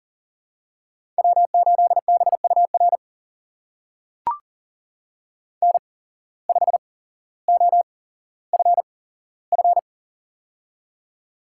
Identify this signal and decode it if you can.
Morse code, heard as a keyed tone, about 39 wpm, spaced as for 17 wpm, 700 Hz. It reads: W8BUR N5OFF
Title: USA Call Signs - 5 Characters - 8x Character Spacing - 40wpm